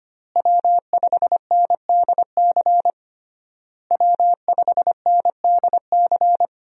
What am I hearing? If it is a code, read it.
Morse code, heard as a keyed tone, about 25 wpm, 700 Hz. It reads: W5NDC W5NDC
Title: USA Call Signs - 5 Characters - Repeated 3x - 25wpm